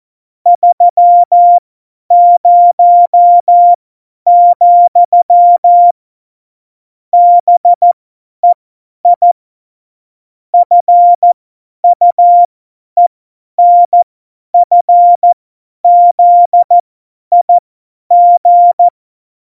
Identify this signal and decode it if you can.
Morse code, heard as a keyed tone, about 14 wpm, spaced as for 10 wpm, 700 Hz. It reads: 30, BEI FUENFZIG
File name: Morsekurs / Witze / Witz 14 Juli 2020, 50 F70